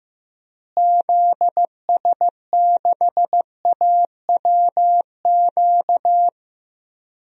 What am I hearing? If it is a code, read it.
Morse code, heard as a keyed tone, about 15 wpm, 700 Hz. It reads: ZS6AWQ
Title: DXCC-100 Call Signs - 6 Characters - Repeated 3x - 15wpm